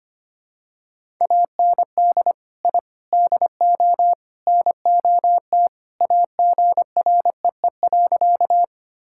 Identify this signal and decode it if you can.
Morse code, heard as a keyed tone, about 25 wpm, 700 Hz. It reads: AND I DO NOT AGREE.
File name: practice / 25wpm / Sentences with 4-Word Phrases - 25wpm